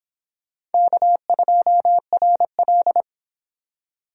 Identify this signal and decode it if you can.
Morse code, heard as a keyed tone, about 30 wpm, 700 Hz. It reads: K2RL